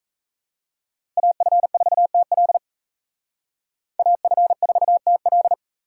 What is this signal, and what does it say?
Morse code, heard as a keyed tone, about 33 wpm, 700 Hz. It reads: AF4TL AF4TL
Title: Speed Racing - USA 2x2 Call Signs 28wpm